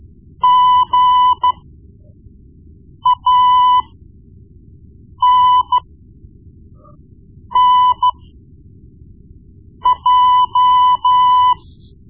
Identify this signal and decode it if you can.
Morse code, heard as a keyed tone, about 13 wpm, spaced as for 5 wpm, 1000 Hz. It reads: GANNJ
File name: Morse Lessons / Morse Lessons / Tape-1B